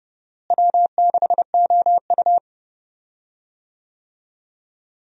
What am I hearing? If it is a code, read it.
Morse code, heard as a keyed tone, about 30 wpm, 700 Hz. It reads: W6OU
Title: USA Call Signs - 4 Characters - Repeated 3x - 30wpm